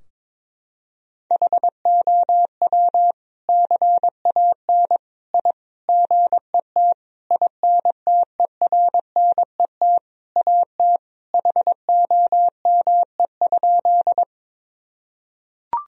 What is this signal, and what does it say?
Morse code, heard as a keyed tone, about 21 wpm, 700 Hz. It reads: HOW CAN I GET INTERNET AT HOME?
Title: Sentences from Top 1000 Words - Review Part 4 - 22wpm